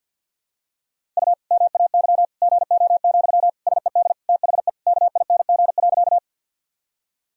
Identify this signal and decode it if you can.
Morse code, heard as a keyed tone, about 49 wpm, 700 Hz. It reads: U MAY GO, SED THE KING.
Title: Sentences from Top 400 Words and Abbreviations 50wpm